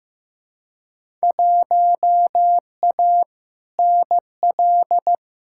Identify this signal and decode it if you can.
Morse code, heard as a keyed tone, about 15 wpm, 700 Hz. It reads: 1A NL
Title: Field Day Exchanges Encore - 15wpm